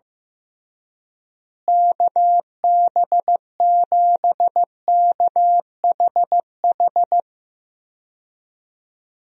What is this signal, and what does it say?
Morse code, heard as a keyed tone, about 15 wpm, 700 Hz. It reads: KB7KHH
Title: USA Call Signs - 6 Characters - Repeated 2x - 15wpm